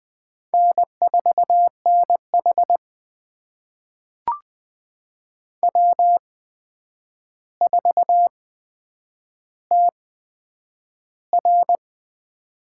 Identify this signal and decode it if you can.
Morse code, heard as a keyed tone, about 20 wpm, spaced as for 9 wpm, 700 Hz. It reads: N4NH W4TR
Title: USA Call Signs - 4 Characters - 8x Character Spacing - 20wpm